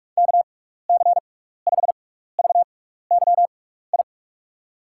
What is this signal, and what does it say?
Morse code, heard as a keyed tone, about 46 wpm, spaced as for 16 wpm, 700 Hz. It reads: KC5VYI
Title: USA Call Signs - 6 Characters - 6x Character Spacing - 45wpm